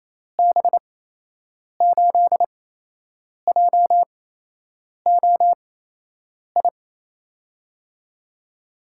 Morse code, heard as a keyed tone, about 27 wpm, spaced as for 8 wpm, 700 Hz. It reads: B8JOI
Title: USA Call Signs - 6 Characters - 8x Character Spacing - 28wpm